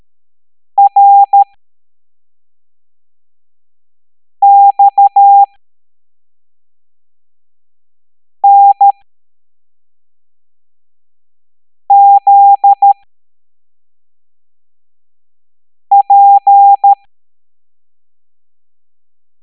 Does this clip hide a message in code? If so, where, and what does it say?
Yes: Morse code from 0.8 to 16.9 s, heard as a keyed tone, about 13 wpm, spaced as for 3 wpm, 800 Hz. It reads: RXNZP